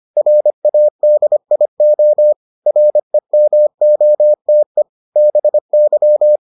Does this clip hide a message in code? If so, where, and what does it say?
Yes: Morse code from 0.2 to 6.4 s, heard as a keyed tone, about 25 wpm, 600 Hz. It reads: RADIO REMOTE BY